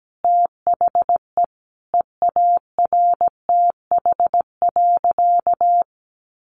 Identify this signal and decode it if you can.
Morse code, heard as a keyed tone, about 17 wpm, 700 Hz. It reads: THE EARTH.
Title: A Sound Of Thunder 17wpm